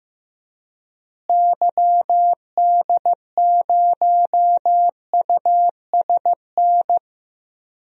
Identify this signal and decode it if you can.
Morse code, heard as a keyed tone, about 15 wpm, 700 Hz. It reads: YD0USN
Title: DXCC-100 Call Signs - 6 Characters - Repeated 3x - 15wpm